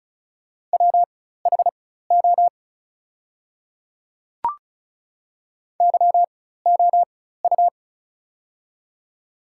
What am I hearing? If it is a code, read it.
Morse code, heard as a keyed tone, about 36 wpm, spaced as for 16 wpm, 700 Hz. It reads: WHO YOU
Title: 4x Character Spacing - Top 100 Words - 35wpm